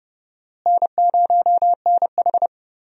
Morse code, heard as a keyed tone, about 30 wpm, 700 Hz. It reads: N0NH
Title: USA Call Signs - 4 Characters - Repeated 2x - 30wpm